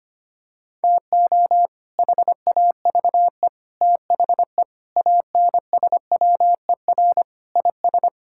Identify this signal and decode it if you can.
Morse code, heard as a keyed tone, about 25 wpm, 700 Hz. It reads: TO HAVE THE ANSWER IS
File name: practice / 25wpm / Sentences with 3-Word Phrases - 25wpm